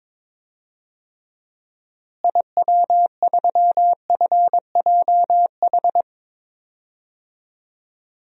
Morse code, heard as a keyed tone, about 22 wpm, 700 Hz. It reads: IW3FJH
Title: DXCC-100 Call Signs - 6 Characters - Repeated 3x - 22wpm